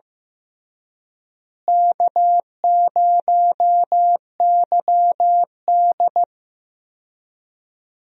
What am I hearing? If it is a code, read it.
Morse code, heard as a keyed tone, about 15 wpm, 700 Hz. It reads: K0YD
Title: USA Call Signs - 4 Characters - Repeated 3x - 15wpm